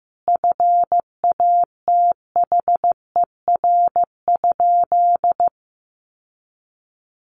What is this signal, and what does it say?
Morse code, heard as a keyed tone, about 14 wpm, 700 Hz. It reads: FATHER?
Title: Sentences from Top 1300 Words 15wpm